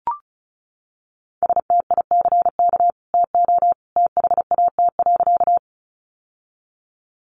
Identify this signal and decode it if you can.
Morse code, heard as a keyed tone, about 34 wpm, 700 Hz. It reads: STICK TO THAT.
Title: Sentences from Top 1200 Words 35wpm